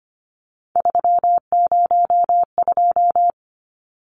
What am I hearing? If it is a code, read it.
Morse code, heard as a keyed tone, about 25 wpm, 700 Hz. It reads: 302
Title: Numbers - 3-Digit 25wpm